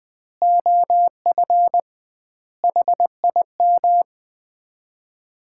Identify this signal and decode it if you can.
Morse code, heard as a keyed tone, about 20 wpm, 700 Hz. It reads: OF HIM
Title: 2x Word-Spacing - Sets of 4 Words - 20wpm